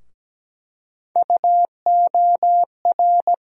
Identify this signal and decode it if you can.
Morse code, heard as a keyed tone, about 17 wpm, 700 Hz. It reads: UOR
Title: USA Call Sign Postfixes - 3 Letters 17wpm